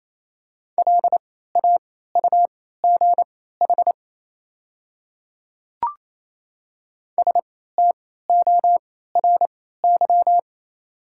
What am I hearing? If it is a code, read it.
Morse code, heard as a keyed tone, about 28 wpm, spaced as for 16 wpm, 700 Hz. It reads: LAUGH STORY
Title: 3x Character Spacing - Top 500 Words - 28wpm